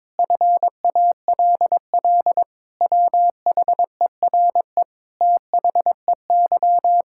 Morse code, heard as a keyed tone, about 22 wpm, 700 Hz. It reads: FALL WHERE THEY
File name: practice / 22wpm / Sets of 8 Words - 22wpm